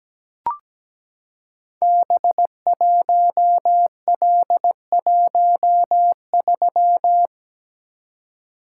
Morse code, heard as a keyed tone, about 17 wpm, 700 Hz. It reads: B1L13
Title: Code Groups - 5 Letter:Numbers - 17wpm